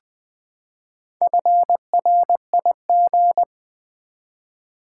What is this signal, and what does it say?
Morse code, heard as a keyed tone, about 20 wpm, 700 Hz. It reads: FRIG
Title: Build A Word - 4 Letters At A Time - 8-Letter Words - 20wpm